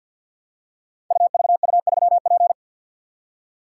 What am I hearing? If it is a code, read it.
Morse code, heard as a keyed tone, about 52 wpm, 700 Hz. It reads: UVU3P
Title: Code Groups - 5 Letter:Numbers - 50wpm